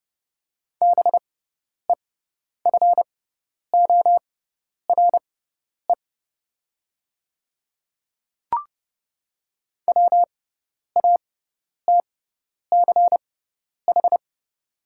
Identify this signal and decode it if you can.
Morse code, heard as a keyed tone, about 30 wpm, spaced as for 10 wpm, 700 Hz. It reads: BEFORE WATCH
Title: 6x Character Spacing - Top 300 Words - 30wpm